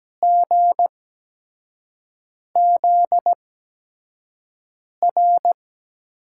Morse code, heard as a keyed tone, about 17 wpm, spaced as for 5 wpm, 700 Hz. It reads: GZR
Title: USA Call Signs - 5 Characters - 8x Character Spacing - 17wpm